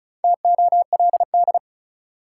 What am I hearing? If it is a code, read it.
Morse code, heard as a keyed tone, about 36 wpm, 700 Hz. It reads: TOLD
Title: Sentences from Top 400 Words - New Word Drill - 35wpm